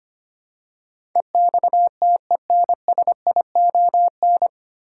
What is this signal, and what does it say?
Morse code, heard as a keyed tone, about 25 wpm, 700 Hz. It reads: EXTENSION
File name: Nine Letter Words - 25wpm